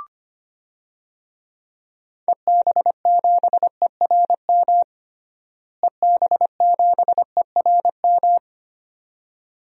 Morse code, heard as a keyed tone, about 25 wpm, 700 Hz. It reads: EB7ERM EB7ERM
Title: DXCC-100 Call Signs - 6 Characters - Repeated 2x - 25wpm